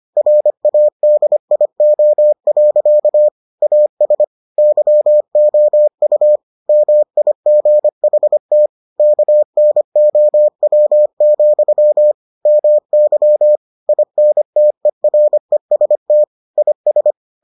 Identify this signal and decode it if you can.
Morse code, heard as a keyed tone, about 25 wpm, 600 Hz. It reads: RADIO. AS YOU MIGHT KNOW, MY INTEREST IS